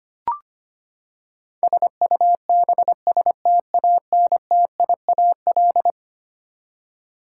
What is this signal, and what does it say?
Morse code, heard as a keyed tone, about 25 wpm, 700 Hz. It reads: SUBSTANTIAL
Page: morse-code-ninja-practice-25wpm